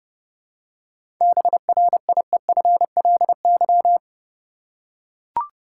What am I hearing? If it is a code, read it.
Morse code, heard as a keyed tone, about 30 wpm, 700 Hz. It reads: BRIEFLY
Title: Seven Letter Words - 30wpm